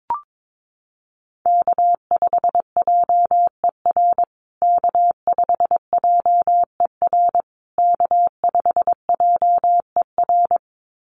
Morse code, heard as a keyed tone, about 22 wpm, 700 Hz. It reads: K5JER K5JER K5JER
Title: USA Call Signs Repeated 3x 22wpm